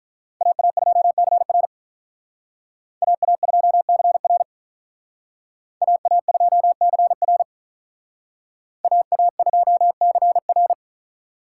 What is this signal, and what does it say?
Morse code, heard as a keyed tone, about 43 wpm, 700 Hz. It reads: AA2CR AA2CR AA2CR AA2CR
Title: Speed Racing - USA 2x2 Call Signs 35wpm